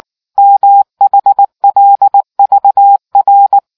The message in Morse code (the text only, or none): MHLVR